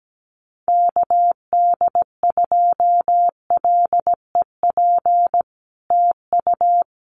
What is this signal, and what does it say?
Morse code, heard as a keyed tone, about 17 wpm, 700 Hz. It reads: KD2LEP TU
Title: Common QSO Phrases Encore 17wpm